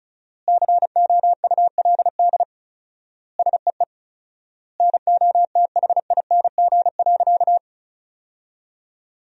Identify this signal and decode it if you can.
Morse code, heard as a keyed tone, about 36 wpm, 700 Hz. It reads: COULD SEE NOTHING.
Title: 4x Word-Spacing - Sentences from Top 500 Words - 35wpm